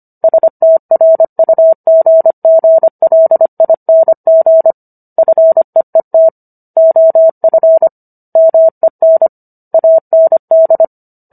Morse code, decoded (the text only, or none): STRUGGLING FEET OF MEN AND